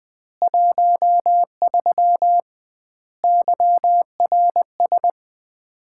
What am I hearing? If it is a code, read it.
Morse code, heard as a keyed tone, about 20 wpm, 700 Hz. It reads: 13 YRS
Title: Common QSO Phrases - 2x Word-Spacing - 20wpm